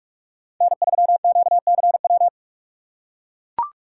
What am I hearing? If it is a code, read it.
Morse code, heard as a keyed tone, about 43 wpm, 700 Hz. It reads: N3QCW